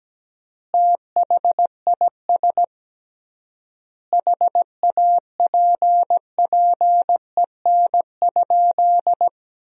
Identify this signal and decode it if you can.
Morse code, heard as a keyed tone, about 17 wpm, 700 Hz. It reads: THIS HAPPEN?
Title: 3x Word-Spacing - Sentences from Top 400 Words - 17wpm